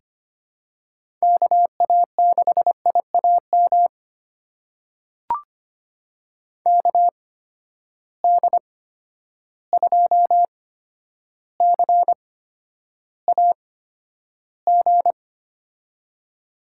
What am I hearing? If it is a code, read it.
Morse code, heard as a keyed tone, about 25 wpm, spaced as for 11 wpm, 700 Hz. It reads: KA6IAM KD2CAG
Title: USA Call Signs - 6 Characters - 8x Character Spacing - 25wpm